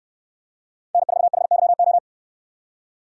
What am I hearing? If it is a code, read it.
Morse code, heard as a keyed tone, about 67 wpm, 700 Hz. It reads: N4HYO